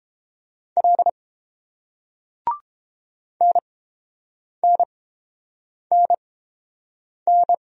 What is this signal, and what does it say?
Morse code, heard as a keyed tone, about 30 wpm, spaced as for 7 wpm, 690 Hz. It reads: L NNNN